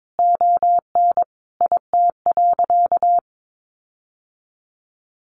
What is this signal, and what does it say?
Morse code, heard as a keyed tone, about 22 wpm, 700 Hz. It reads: ON IT.
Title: Sentences from Top 1400 Words 22wpm